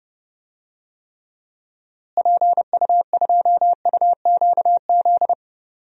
Morse code, heard as a keyed tone, about 30 wpm, 700 Hz. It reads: PU2UQZ
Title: DXCC-100 Call Signs - 6 Characters - Repeated 3x - 30wpm